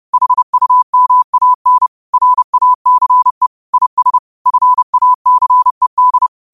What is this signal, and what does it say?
Morse code, heard as a keyed tone, about 30 wpm, 1000 Hz. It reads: HUMAN RACE IS FACED
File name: qotdfiles / 30wpm